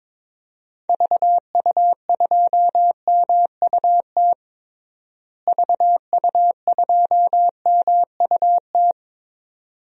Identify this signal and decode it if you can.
Morse code, heard as a keyed tone, about 22 wpm, 700 Hz. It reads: VU2MUT VU2MUT
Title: DXCC-100 Call Signs - 6 Characters - Repeated 2x - 22wpm